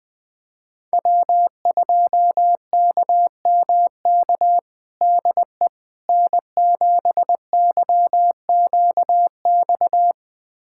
Morse code, heard as a keyed tone, about 21 wpm, 700 Hz. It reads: W2KMK DE N7YQX
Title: USA Callsign DE Callsign - 5 Characters - 20wpm